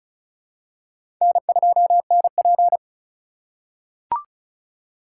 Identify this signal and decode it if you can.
Morse code, heard as a keyed tone, about 36 wpm, 700 Hz. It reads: N2NP
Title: USA Call Signs - 4x Character Spacing - 35wpm